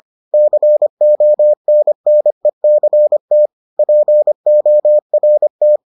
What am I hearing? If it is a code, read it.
Morse code, heard as a keyed tone, about 25 wpm, 600 Hz. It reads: CONNECT PORT